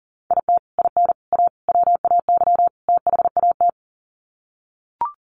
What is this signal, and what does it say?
Morse code, heard as a keyed tone, about 40 wpm, 700 Hz. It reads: IT IN A WAY THAT
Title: Sets of 5 Words Encore2 40wpm